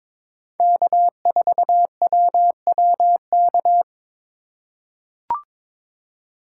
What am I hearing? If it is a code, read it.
Morse code, heard as a keyed tone, about 22 wpm, 700 Hz. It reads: K4WWK
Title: USA Call Signs - 5 Characters - 2x Character Spacing - 22wpm